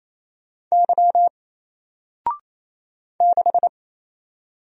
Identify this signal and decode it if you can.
Morse code, heard as a keyed tone, about 27 wpm, 700 Hz. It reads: Y 6
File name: Single-Letter-Number-28wpm